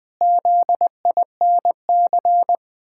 Morse code, heard as a keyed tone, about 20 wpm, 700 Hz. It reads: ZINC
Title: Pangram Challenge v2 - 20wpm